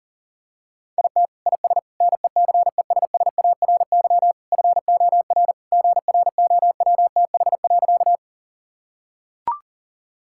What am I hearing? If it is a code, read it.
Morse code, heard as a keyed tone, about 40 wpm, 700 Hz. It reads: IT IS NECESSARY FOR GROWTH.
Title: Sentences from Top 1000 Words - Review Part 5 - 40wpm